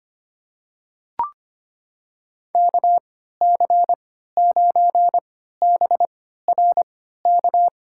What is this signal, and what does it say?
Morse code, heard as a keyed tone, about 25 wpm, spaced as for 14 wpm, 700 Hz. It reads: KC9BRK
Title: USA Call Signs - 6 Characters - 3x Character Spacing - 25wpm